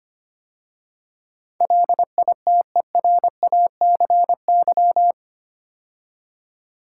Morse code, heard as a keyed tone, about 25 wpm, 700 Hz. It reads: LITERACY
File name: 8 Letter Words - First 3 Letters Given - 25wpm